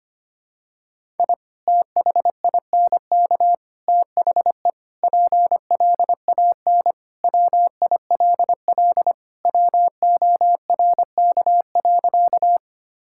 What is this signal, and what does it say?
Morse code, heard as a keyed tone, about 25 wpm, 700 Hz. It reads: I THINK THE PLAN WILL WORK.
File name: practice / 25wpm / Sentences with 3-Word Phrases - 25wpm